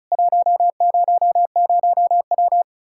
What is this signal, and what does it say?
Morse code, heard as a keyed tone, about 47 wpm, spaced as for 41 wpm, 700 Hz. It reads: 100W
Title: Speed-Racing Common QSO Phrases Encore - 35wpm